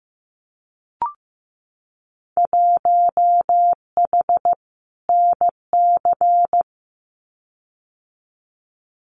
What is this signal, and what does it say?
Morse code, heard as a keyed tone, about 15 wpm, 700 Hz. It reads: 1H NC
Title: Winter Field Day Exchanges 15wpm